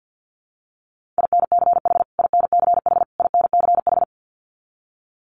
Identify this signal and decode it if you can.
Morse code, heard as a keyed tone, about 50 wpm, 700 Hz. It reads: INCH INCH INCH
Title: Top 500 Words Repeated 50wpm